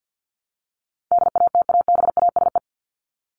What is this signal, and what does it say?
Morse code, heard as a keyed tone, about 50 wpm, 700 Hz. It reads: DATABASE
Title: Top 1751-2000 Words 50wpm